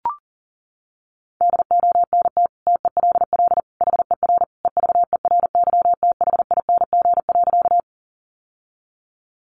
Morse code, heard as a keyed tone, about 40 wpm, 700 Hz. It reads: DONT TELL HER EVERYTHING.
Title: Sentences from Top 700 Words 40wpm